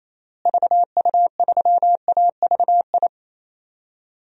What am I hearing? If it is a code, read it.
Morse code, heard as a keyed tone, about 28 wpm, 700 Hz. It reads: VU3AVI
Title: DXCC-100 Call Signs - 6 Characters - Repeated 2x - 28wpm